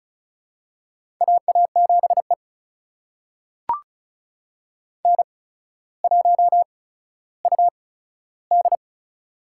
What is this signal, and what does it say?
Morse code, heard as a keyed tone, about 36 wpm, spaced as for 15 wpm, 700 Hz. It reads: AA7E N1UD
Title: USA Call Signs - 4 Characters - 8x Character Spacing - 35wpm